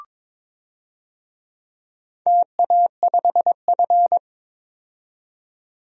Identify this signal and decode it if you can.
Morse code, heard as a keyed tone, about 22 wpm, 700 Hz. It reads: TA5F
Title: DXCC-100 Call Signs - 4 Characters - 22wpm